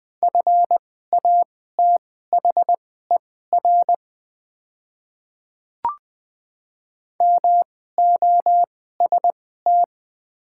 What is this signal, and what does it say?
Morse code, heard as a keyed tone, about 20 wpm, spaced as for 15 wpm, 700 Hz. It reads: FATHER MOST
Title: 2x Character Spacing - Top 300 Words - 20wpm